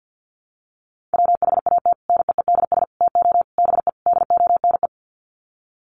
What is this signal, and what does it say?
Morse code, heard as a keyed tone, about 50 wpm, 700 Hz. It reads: WHAT NEEDS TO BE DONE
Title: Sets of 5 Words Encore 50wpm